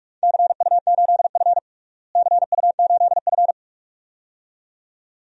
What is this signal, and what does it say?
Morse code, heard as a keyed tone, about 41 wpm, 700 Hz. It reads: CU8F CU8F
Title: DXCC-100 Call Signs - 4 Characters - Repeated 2x - 45wpm